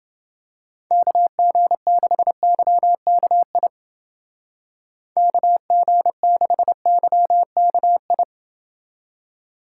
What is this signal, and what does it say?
Morse code, heard as a keyed tone, about 29 wpm, 700 Hz. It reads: KG6YKI KG6YKI